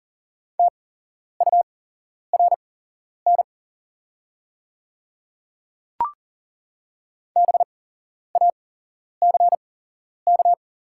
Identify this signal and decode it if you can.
Morse code, heard as a keyed tone, about 40 wpm, spaced as for 11 wpm, 700 Hz. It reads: TURN BACK
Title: 8x Character Spacing - Top 200 Words - 40wpm